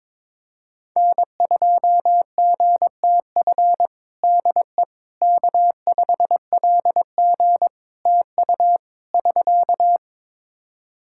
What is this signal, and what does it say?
Morse code, heard as a keyed tone, about 22 wpm, 700 Hz. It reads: N2GTF DE K5LG TU <SK>
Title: Practice QSO - Exchange 4 - Ending - 22wpm